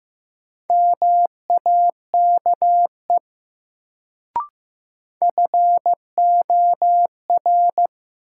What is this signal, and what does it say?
Morse code, heard as a keyed tone, about 15 wpm, 700 Hz. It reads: MAKE FOR
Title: Sentences from Top 100 Words - New Word Drill - 15wpm